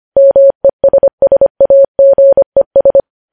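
Morse code, heard as a keyed tone, about 25 wpm, 550 Hz. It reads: MESSAGES
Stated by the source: Morse code news headlines sent at 25 wpm.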